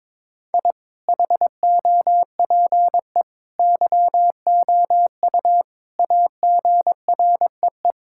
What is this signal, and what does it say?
Morse code, heard as a keyed tone, about 22 wpm, 700 Hz. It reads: I HOPE YOU AGREE
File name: practice / 22wpm / Sentences with 8-Word Phrases - 22wpm